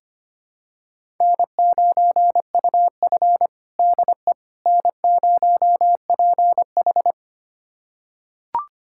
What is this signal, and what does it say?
Morse code, heard as a keyed tone, about 25 wpm, 700 Hz. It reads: N9UF DE N0PH
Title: USA Callsign DE Callsign - 4 Characters - 25wpm